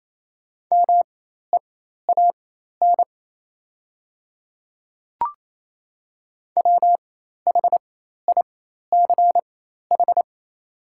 Morse code, heard as a keyed tone, about 28 wpm, spaced as for 13 wpm, 690 Hz. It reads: MEAN WHICH
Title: 4x Character Spacing - Top 500 Words - 28wpm